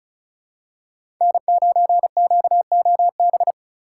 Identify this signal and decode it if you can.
Morse code, heard as a keyed tone, about 34 wpm, 700 Hz. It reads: N9QOB
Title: USA Call Signs - 5 Characters - 4x Character Spacing - 35wpm